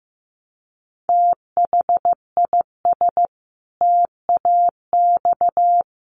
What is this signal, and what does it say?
Morse code, heard as a keyed tone, about 15 wpm, 700 Hz. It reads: THIS TAX
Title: Sentences from Top 1500 Words 15wpm